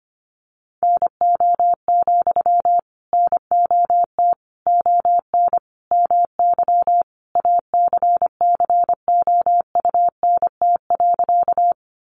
Morse code, heard as a keyed tone, about 25 wpm, 700 Hz. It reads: NO, NOT ON MY ACCOUNT.
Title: Sentences from Top 900 Words 25wpm